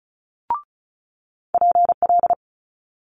Morse code, heard as a keyed tone, about 35 wpm, 700 Hz. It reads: PL
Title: Multi-letter Phonemes Quick Fire 35wpm